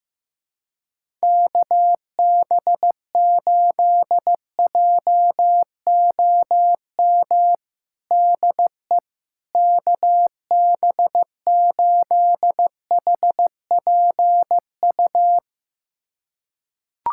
Morse code, heard as a keyed tone, about 15 wpm, 700 Hz. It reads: KB8JOM DE KB8HPU
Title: USA Callsign DE Callsign - 6 Characters - 15wpm